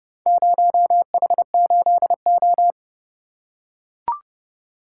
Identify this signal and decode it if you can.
Morse code, heard as a keyed tone, about 28 wpm, 700 Hz. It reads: 0H8O